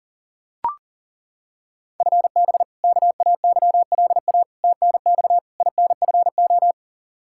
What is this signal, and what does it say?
Morse code, heard as a keyed tone, about 40 wpm, 700 Hz. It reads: FB KAYLA TNX INFO
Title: K1USN SST QSO Practice 40wpm